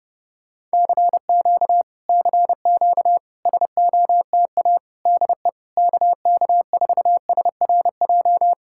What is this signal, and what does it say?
Morse code, heard as a keyed tone, about 30 wpm, 700 Hz. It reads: CQ CQ SOTA DE KK4SRJ